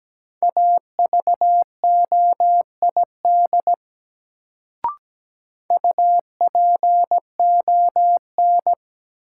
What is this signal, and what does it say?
Morse code, heard as a keyed tone, about 17 wpm, 700 Hz. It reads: AVOID UPON
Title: Sentences from Top 1000 Words - Review Part 6 - New Word Drill - 17wpm